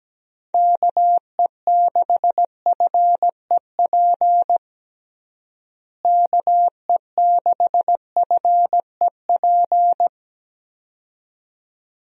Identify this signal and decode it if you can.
Morse code, heard as a keyed tone, about 17 wpm, 700 Hz. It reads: KE6FEP KE6FEP
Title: USA Call Signs - 6 Characters - Repeated 2x - 17wpm